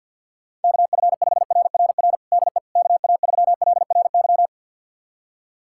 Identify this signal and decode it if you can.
Morse code, heard as a keyed tone, about 46 wpm, 700 Hz. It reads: KF5RRR DE KA3LRY
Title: USA Callsign DE Callsign - 6 Characters - 50wpm